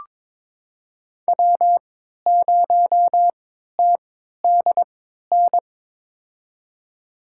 Morse code, heard as a keyed tone, about 22 wpm, spaced as for 13 wpm, 700 Hz. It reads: W0TDN
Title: USA Call Signs - 5 Characters - 3x Character Spacing - 22wpm